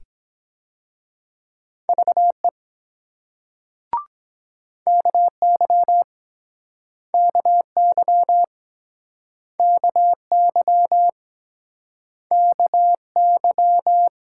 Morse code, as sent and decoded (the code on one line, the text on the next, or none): ...- . / -.- -.-- / -.- -.-- / -.- -.-- / -.- -.--
VE KY KY KY KY